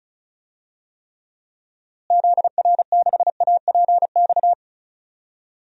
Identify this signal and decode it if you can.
Morse code, heard as a keyed tone, about 36 wpm, 700 Hz. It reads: ZR6APX
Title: DXCC-100 Call Signs - 6 Characters - Repeated 2x - 35wpm